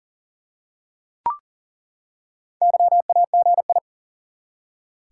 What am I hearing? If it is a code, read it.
Morse code, heard as a keyed tone, about 40 wpm, 700 Hz. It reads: YAGI
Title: Top 180 QSO Elements 40wpm